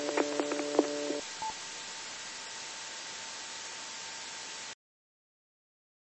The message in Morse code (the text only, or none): none